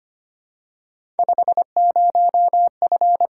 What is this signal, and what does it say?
Morse code, heard as a keyed tone, about 25 wpm, 700 Hz. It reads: 50F
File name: Three Letter-Number Combinations - 25wpm